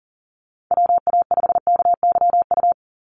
Morse code, heard as a keyed tone, about 41 wpm, 700 Hz. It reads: WA5KYU
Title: USA Call Signs 40wpm